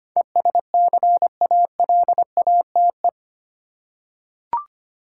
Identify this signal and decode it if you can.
Morse code, heard as a keyed tone, about 25 wpm, 700 Hz. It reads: ESCALATE
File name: Eight Letter Words - 25wpm